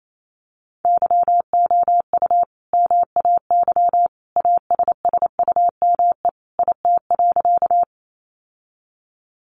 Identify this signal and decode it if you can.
Morse code, heard as a keyed tone, about 28 wpm, 700 Hz. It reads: YOU MAY ASSUME IT.
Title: Sentences from Top 1500 Words 28wpm